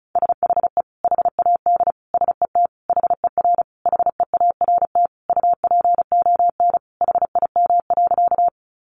Morse code, heard as a keyed tone, about 34 wpm, 700 Hz. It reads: SHE HAD SET HER HEART UPON HIM.
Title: Sentences from Top 1200 Words 35wpm